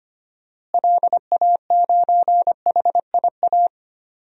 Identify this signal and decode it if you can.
Morse code, heard as a keyed tone, about 25 wpm, 700 Hz. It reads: LA9HIA
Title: DXCC-100 Call Signs - 6 Characters - Repeated 3x - 25wpm